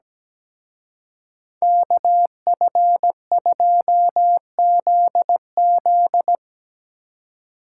Morse code, heard as a keyed tone, about 19 wpm, 700 Hz. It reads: KF2ZZ
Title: Speed Racing - USA 2x2 Call Signs 17wpm